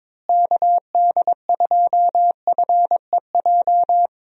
Morse code, heard as a keyed tone, about 22 wpm, 700 Hz. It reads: KD2FEJ
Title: USA Call Signs - 6 Characters - Repeated 2x - 22wpm